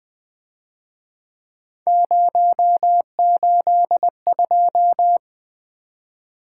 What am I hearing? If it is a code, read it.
Morse code, heard as a keyed tone, about 23 wpm, 700 Hz. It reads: 082
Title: Speed-Racing - Three Letter-Number Combinations - 20wpm